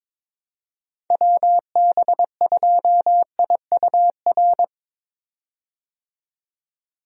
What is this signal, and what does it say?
Morse code, heard as a keyed tone, about 22 wpm, 700 Hz. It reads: WB2IUR
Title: USA Call Signs - 6 Characters - 22wpm